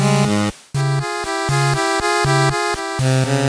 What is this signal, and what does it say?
There is no code.